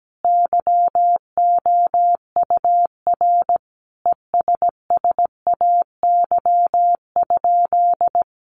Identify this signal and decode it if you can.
Morse code, heard as a keyed tone, about 17 wpm, 700 Hz. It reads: YOUR ESSAY?
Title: Sentences from Top 1500 Words 17wpm